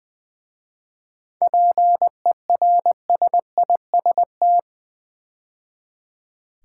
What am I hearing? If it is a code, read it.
Morse code, heard as a keyed tone, about 20 wpm, 700 Hz. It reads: PERSIST
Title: 7 Letter Words - First 3 Letters Given - 20wpm